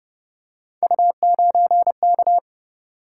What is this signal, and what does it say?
Morse code, heard as a keyed tone, about 30 wpm, 700 Hz. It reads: U9K